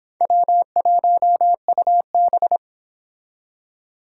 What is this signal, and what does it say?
Morse code, heard as a keyed tone, about 30 wpm, 700 Hz. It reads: W1UB